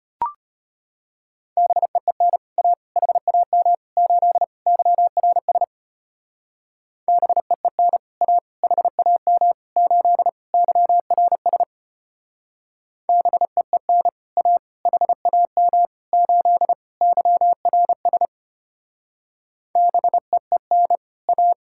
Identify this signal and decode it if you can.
Morse code, heard as a keyed tone, about 32 wpm, 700 Hz. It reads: BEEN A HAM 8 YRS BEEN A HAM 8 YRS BEEN A HAM 8 YRS BEEN A